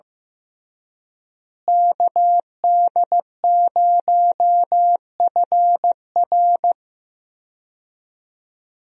Morse code, heard as a keyed tone, about 15 wpm, 700 Hz. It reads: KD0FR
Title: USA Call Signs - 5 Characters - Repeated 3x - 15wpm